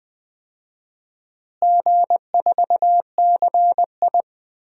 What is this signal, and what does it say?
Morse code, heard as a keyed tone, about 20 wpm, 700 Hz. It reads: G4CI
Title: DXCC-100 Call Signs - 4 Characters - Repeated 2x - 20wpm